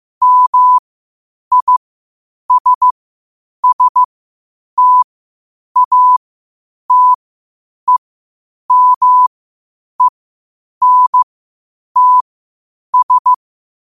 Morse code, heard as a keyed tone, about 15 wpm, spaced as for 8 wpm, 1000 Hz. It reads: MISSTATEMENTS
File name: qotdfiles / 8wpm